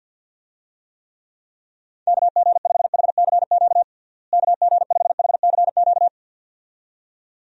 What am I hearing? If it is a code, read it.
Morse code, heard as a keyed tone, about 49 wpm, 700 Hz. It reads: KG5HCQ KG5HCQ
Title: USA Call Signs - 6 Characters - Repeated 2x - 50wpm